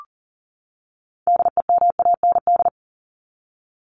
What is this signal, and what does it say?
Morse code, heard as a keyed tone, about 40 wpm, 700 Hz. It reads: DEMAND
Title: Top 501-750 Words 40wpm